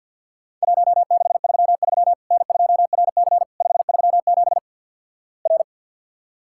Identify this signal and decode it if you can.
Morse code, heard as a keyed tone, about 50 wpm, 700 Hz. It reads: 1633 N2RC 537 R
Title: Worked All Europe DX Contest - QTC Exchange with Repeats - 50wpm